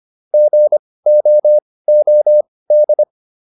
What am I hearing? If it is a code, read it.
Morse code, heard as a keyed tone, about 25 wpm, spaced as for 18 wpm, 600 Hz. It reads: GOOD
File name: QSO408 18